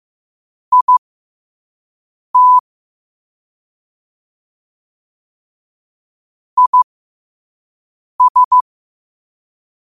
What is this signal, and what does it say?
Morse code, heard as a keyed tone, about 15 wpm, spaced as for 5 wpm, 1000 Hz. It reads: IT IS
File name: qotdfiles / 5wpm